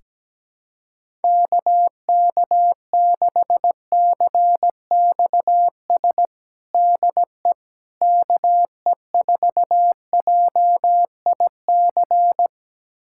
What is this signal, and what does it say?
Morse code, heard as a keyed tone, about 17 wpm, 700 Hz. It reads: KK6CXS DE KE4JIC
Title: USA Callsign DE Callsign - 6 Characters - 17wpm